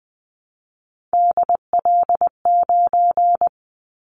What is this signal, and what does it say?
Morse code, heard as a keyed tone, about 20 wpm, 700 Hz. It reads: DL9
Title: Callsign Prefixes for Top 50 DXCC - 20wpm